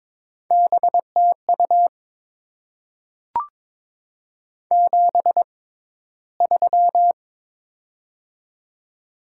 Morse code, heard as a keyed tone, about 22 wpm, spaced as for 14 wpm, 700 Hz. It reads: BTU 73
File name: Top 20 QSO Elements - 6x Character Spacing - 22wpm